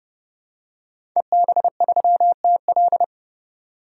Morse code, heard as a keyed tone, about 30 wpm, 700 Hz. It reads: EB3TL
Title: DXCC-100 Call Signs - 5 Characters - 30wpm